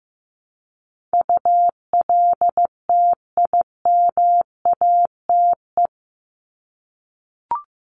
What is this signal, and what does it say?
Morse code, heard as a keyed tone, about 15 wpm, 700 Hz. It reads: ULTIMATE
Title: LY Suffix - Top 100 Words with Base Words 15wpm